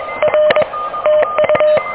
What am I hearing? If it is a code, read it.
Morse code, heard as a keyed tone, about 23 wpm, 610 Hz. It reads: R TU
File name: OK QRP 2023 / OK QRP 2023